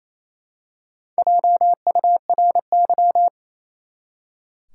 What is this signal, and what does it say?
Morse code, heard as a keyed tone, about 28 wpm, 700 Hz. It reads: JURY
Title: Four Letter Words - 28wpm